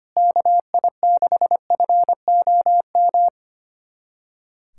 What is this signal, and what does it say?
Morse code, heard as a keyed tone, about 25 wpm, 700 Hz. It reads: KI6FOM
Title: USA Callsign DE Callsign - 6 Characters - 25wpm